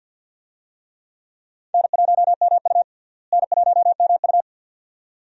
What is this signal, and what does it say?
Morse code, heard as a keyed tone, about 52 wpm, 700 Hz. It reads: N1MU N1MU
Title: USA Call Signs - 4 Characters - Repeated 2x - 50wpm